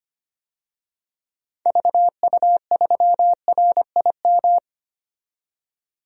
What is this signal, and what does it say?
Morse code, heard as a keyed tone, about 25 wpm, 700 Hz. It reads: VU3RIM
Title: DXCC-100 Call Signs - 6 Characters - 25wpm